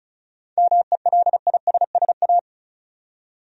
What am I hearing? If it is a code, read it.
Morse code, heard as a keyed tone, about 36 wpm, 700 Hz. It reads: MELISSA
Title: USA Common Names - 35wpm